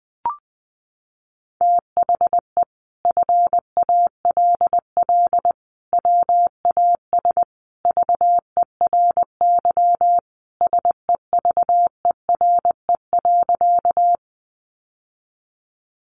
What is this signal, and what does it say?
Morse code, heard as a keyed tone, about 20 wpm, 700 Hz. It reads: THE FALL WAS VERY SEVERE.
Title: Sentences from Top 1400 Words 20wpm